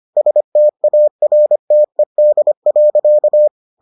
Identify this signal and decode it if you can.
Morse code, heard as a keyed tone, about 25 wpm, 600 Hz. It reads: STARTED.